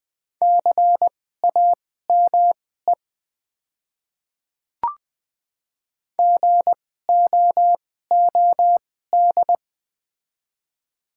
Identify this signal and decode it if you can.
Morse code, heard as a keyed tone, about 20 wpm, spaced as for 15 wpm, 700 Hz. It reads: CAME GOOD
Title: 2x Character Spacing - Top 200 Words - 20wpm